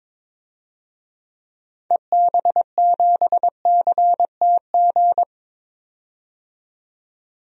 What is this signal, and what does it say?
Morse code, heard as a keyed tone, about 22 wpm, 700 Hz. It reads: EB7CTG
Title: DXCC-100 Call Signs - 6 Characters - 22wpm